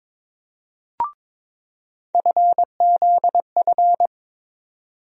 Morse code, heard as a keyed tone, about 22 wpm, 700 Hz. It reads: FZF
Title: USA Call Sign Postfixes - 3 Letters 22wpm